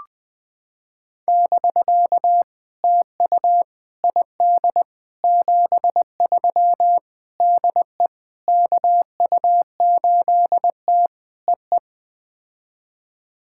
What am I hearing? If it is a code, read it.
Morse code, heard as a keyed tone, about 21 wpm, 700 Hz. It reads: <BK> TU ID 73 DE KU8T EE
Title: POTA QSO Practice - 20wpm